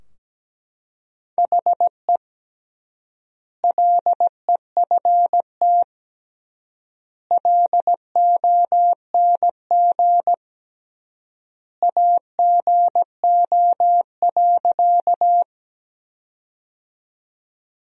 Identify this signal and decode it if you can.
Morse code, heard as a keyed tone, about 17 wpm, 700 Hz. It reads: HE LEFT LONG AGO.
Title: 3x Word-Spacing - Sentences from Top 500 Words - 17wpm